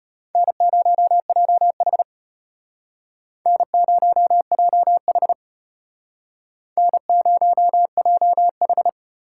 Morse code, heard as a keyed tone, about 32 wpm, 700 Hz. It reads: N0JH N0JH N0JH